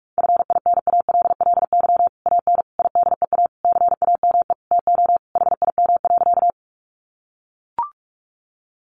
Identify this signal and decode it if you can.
Morse code, heard as a keyed tone, about 44 wpm, 700 Hz. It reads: FINALLY AN IDEA CAME TO HIM.
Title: Sentences from Top 1000 Words 45wpm